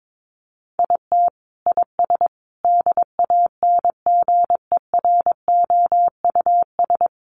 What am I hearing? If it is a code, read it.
Morse code, heard as a keyed tone, about 22 wpm, 700 Hz. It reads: IT IS DANGEROUS
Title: Sentences from Top 1200 Words 22wpm